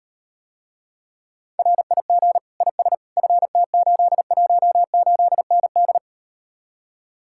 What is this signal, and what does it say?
Morse code, heard as a keyed tone, about 38 wpm, 700 Hz. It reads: RIG IS FT818ND